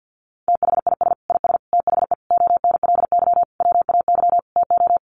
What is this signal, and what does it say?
Morse code, heard as a keyed tone, about 48 wpm, 700 Hz. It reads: THIS IS THE ONLY WAY TO
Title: Sentences from Top 1400 Words 50wpm